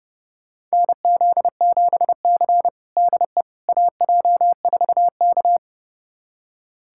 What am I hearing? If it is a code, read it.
Morse code, heard as a keyed tone, about 32 wpm, 700 Hz. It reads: NZ7C DE AJ4K
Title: USA Callsign DE Callsign - 4 Characters - 30wpm